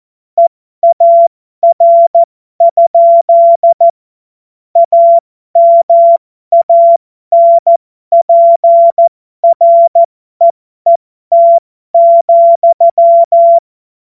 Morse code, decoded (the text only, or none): EAR? AMANPREET,